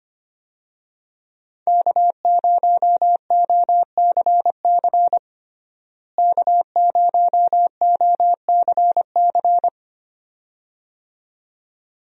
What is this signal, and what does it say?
Morse code, heard as a keyed tone, about 25 wpm, 700 Hz. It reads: K0OCC K0OCC
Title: USA Call Signs - 5 Characters - Repeated 2x - 25wpm